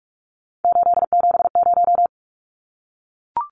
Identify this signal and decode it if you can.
Morse code, heard as a keyed tone, about 44 wpm, 700 Hz. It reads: 870
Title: Numbers - 3-Digit 45wpm